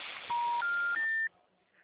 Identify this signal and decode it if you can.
Special information tone, with noise.
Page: Atibaia - SP 1